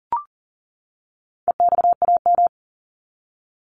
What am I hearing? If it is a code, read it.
Morse code, heard as a keyed tone, about 40 wpm, 700 Hz. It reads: EXAM
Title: Top 501-750 Words 40wpm